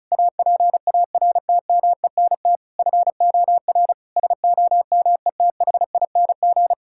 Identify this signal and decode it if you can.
Morse code, heard as a keyed tone, about 36 wpm, 700 Hz. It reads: APARTMENT FOR SOMETHING
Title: Sentences from Top 1000 Words - Review Part 6 - 35wpm